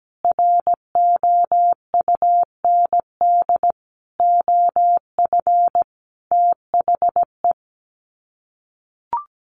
Morse code, heard as a keyed tone, about 17 wpm, 700 Hz. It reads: ROUND OF THE
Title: Sets of 5 Words Encore 17wpm